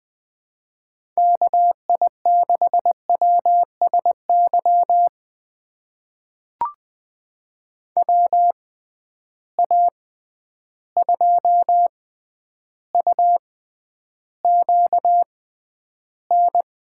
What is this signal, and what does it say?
Morse code, heard as a keyed tone, about 20 wpm, spaced as for 10 wpm, 700 Hz. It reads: KI6WSY WA2UQN
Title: USA Call Signs - 6 Characters - 6x Character Spacing - 20wpm